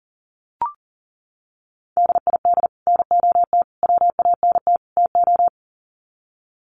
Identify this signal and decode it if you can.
Morse code, heard as a keyed tone, about 40 wpm, 700 Hz. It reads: DID NOT WANT TO
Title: Sets of 4 Words 40wpm